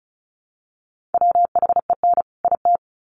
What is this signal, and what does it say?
Morse code, heard as a keyed tone, about 34 wpm, 700 Hz. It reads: WHEN IT
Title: Sets of 2 Words 35wpm